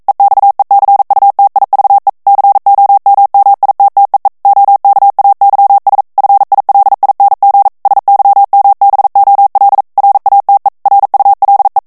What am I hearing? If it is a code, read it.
Morse code, heard as a keyed tone, about 42 wpm, 800 Hz. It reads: EXECUTIVE COMMITTEE OKAYS FILING SYMBOL RATE RULE
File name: 40 mots minute